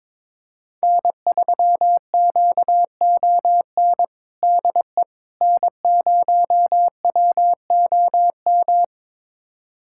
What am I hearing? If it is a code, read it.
Morse code, heard as a keyed tone, about 22 wpm, 700 Hz. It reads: N3QON DE N0WOM
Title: USA Callsign DE Callsign - 5 Characters - 22wpm